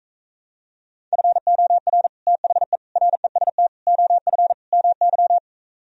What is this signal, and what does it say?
Morse code, heard as a keyed tone, about 38 wpm, 700 Hz. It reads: FOR THE REST OF MY